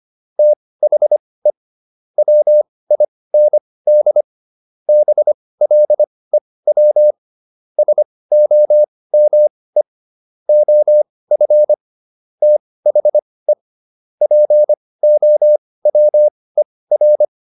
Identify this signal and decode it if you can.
Morse code, heard as a keyed tone, about 25 wpm, spaced as for 18 wpm, 600 Hz. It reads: THE WIND BLEW SOME OF THE POWER